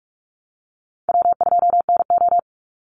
Morse code, heard as a keyed tone, about 46 wpm, 700 Hz. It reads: W2NO
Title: Easy Call Signs - 45wpm